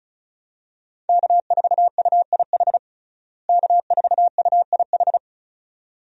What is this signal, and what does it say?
Morse code, heard as a keyed tone, about 34 wpm, 700 Hz. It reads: K4UIH K4UIH
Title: USA Call Signs - 5 Characters - Repeated 2x - 35wpm